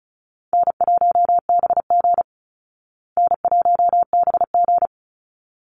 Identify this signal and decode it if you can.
Morse code, heard as a keyed tone, about 34 wpm, 700 Hz. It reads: N1BG N1BG
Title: Easy Call Signs Repeated 2x - 35wpm